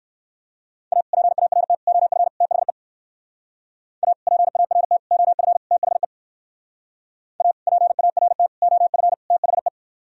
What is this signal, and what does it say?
Morse code, heard as a keyed tone, about 64 wpm, 700 Hz. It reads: A PART OF THE A PART OF THE A PART OF THE